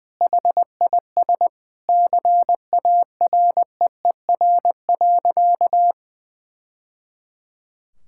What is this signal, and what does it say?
Morse code, heard as a keyed tone, about 19 wpm, 700 Hz. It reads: HIS CAREER.
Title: Sentences from Top 1000 Words - Review Part 6 - 20wpm